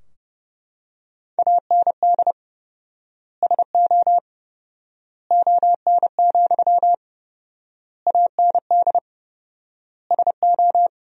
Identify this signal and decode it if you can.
Morse code, heard as a keyed tone, about 30 wpm, 700 Hz. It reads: AND SO ON, AND SO
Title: 4x Word-Spacing - Sentences from Top 100 Words - 30wpm